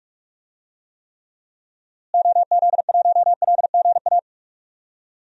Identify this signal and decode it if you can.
Morse code, heard as a keyed tone, about 44 wpm, 700 Hz. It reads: OZ1LGA
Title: DXCC-100 Call Signs - 6 Characters - Repeated 2x - 45wpm